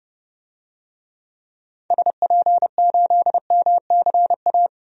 Morse code, heard as a keyed tone, about 30 wpm, 700 Hz. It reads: SP8MCA
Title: DXCC-100 Call Signs - 6 Characters - Repeated 3x - 30wpm